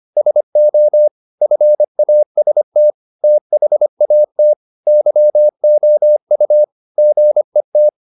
Morse code, heard as a keyed tone, about 25 wpm, 600 Hz. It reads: SO FAST THAT YOU GET